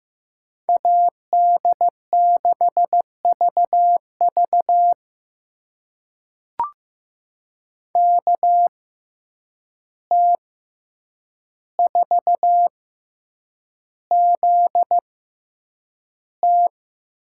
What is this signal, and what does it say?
Morse code, heard as a keyed tone, about 15 wpm, spaced as for 8 wpm, 700 Hz. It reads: AD6VV KT4ZT
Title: USA Call Signs - 5 Characters - 6x Character Spacing - 15wpm